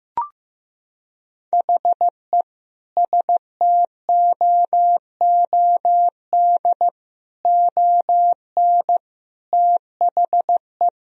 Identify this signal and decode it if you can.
Morse code, heard as a keyed tone, about 15 wpm, 700 Hz. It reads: HE STOOD ON THE
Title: Sentences from Top 1000 Words - Review Part 4 - 15wpm